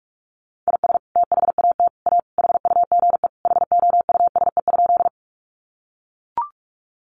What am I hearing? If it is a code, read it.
Morse code, heard as a keyed tone, about 44 wpm, 700 Hz. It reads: IS THAT A HUGE HOUSE?
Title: Sentences from Top 1100 Words 45wpm